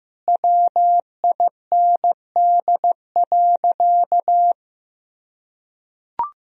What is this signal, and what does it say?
Morse code, heard as a keyed tone, about 15 wpm, 700 Hz. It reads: WIND.
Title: Sentences from Top 1000 Words - Review Part 2 - 15wpm